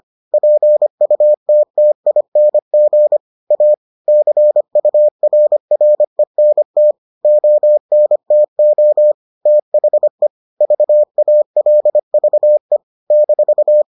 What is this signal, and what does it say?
Morse code, heard as a keyed tone, about 25 wpm, 600 Hz. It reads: PUTTING A CURRENT ONTO THE VALVE -